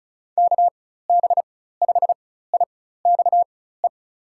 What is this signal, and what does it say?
Morse code, heard as a keyed tone, about 36 wpm, spaced as for 16 wpm, 700 Hz. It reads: KB5IXE
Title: USA Call Signs - 6 Characters - 4x Character Spacing - 35wpm